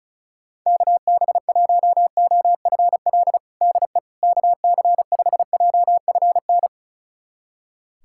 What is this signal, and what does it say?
Morse code, heard as a keyed tone, about 34 wpm, 700 Hz. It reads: KB1OFL DE KC5JFN
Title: USA Callsign DE Callsign - 6 Characters - 35wpm